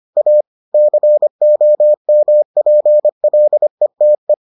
Morse code, heard as a keyed tone, about 25 wpm, 600 Hz. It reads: A COMPLETE